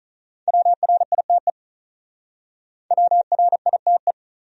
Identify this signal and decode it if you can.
Morse code, heard as a keyed tone, about 46 wpm, 700 Hz. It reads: WRITE WRITE